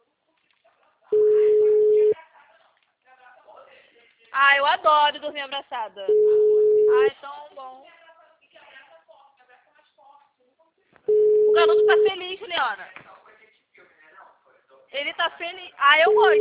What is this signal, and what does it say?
Ringback tone (Europe, 425 Hz), with speech.
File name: out- -2021- - - .688